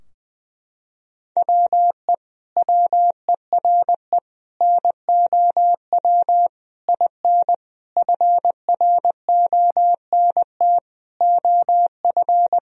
Morse code, heard as a keyed tone, about 20 wpm, 700 Hz. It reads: WE WERE NOW IN FRONT OF
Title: Sentences from Top 1000 Words - Review Part 2 - 20wpm